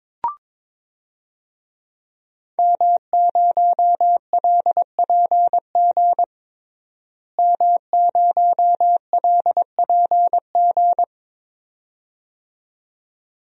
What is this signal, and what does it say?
Morse code, heard as a keyed tone, about 22 wpm, 700 Hz. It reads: M0LPG M0LPG
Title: DXCC-100 Call Signs - 5 Characters - Repeated 2x - 22wpm